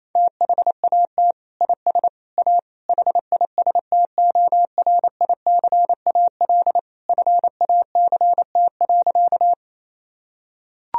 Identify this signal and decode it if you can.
Morse code, heard as a keyed tone, about 28 wpm, 700 Hz. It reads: THAT IS A HISTORICAL FACT.
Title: Sentences from Top 1000 Words - Review Part 4 - 28wpm